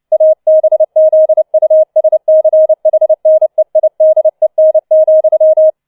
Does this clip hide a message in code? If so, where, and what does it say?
Yes: Morse code from 0.1 to 5.7 s, heard as a keyed tone, about 29 wpm, 630 Hz. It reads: ABZUSCHNEIDEN,